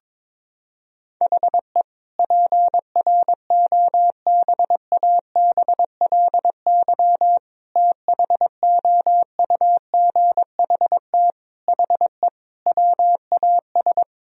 Morse code, heard as a keyed tone, about 22 wpm, 700 Hz. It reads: HE PROBABLY THOUGHT HE WAS
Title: Sentences from Top 1000 Words - Review Part 4 - 22wpm